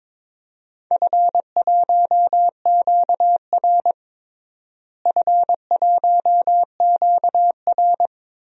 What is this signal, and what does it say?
Morse code, heard as a keyed tone, about 22 wpm, 700 Hz. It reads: F1QR F1QR
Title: DXCC-100 Call Signs - 4 Characters - Repeated 3x - 22wpm